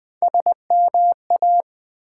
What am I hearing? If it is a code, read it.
Morse code, heard as a keyed tone, about 20 wpm, 700 Hz. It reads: SMA
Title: Build A Word - 1 Letter At A Time - 3 to 5-Letter Words - 20wpm